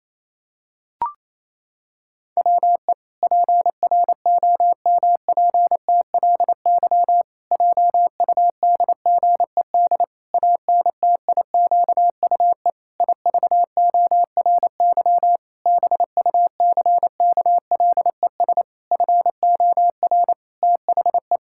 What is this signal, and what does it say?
Morse code, heard as a keyed tone, about 27 wpm, 700 Hz. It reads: WE PROMPTLY JUDGED ANTIQUE IVORY BUCKLES FOR THE
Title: Pangram Challenge v2 - 28wpm